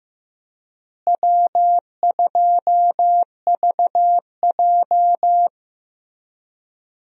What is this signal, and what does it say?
Morse code, heard as a keyed tone, about 15 wpm, 700 Hz. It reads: W2VJ
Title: USA Call Signs - 4 Characters - Repeated 3x - 15wpm